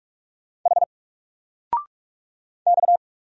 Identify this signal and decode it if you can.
Morse code, heard as a keyed tone, about 46 wpm, 700 Hz. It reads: H X
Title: Single Letter-Number 45wpm